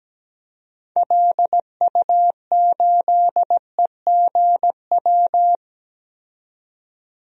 Morse code, heard as a keyed tone, about 17 wpm, 700 Hz. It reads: LU8EGW
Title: DXCC-100 Call Signs - 6 Characters - 17wpm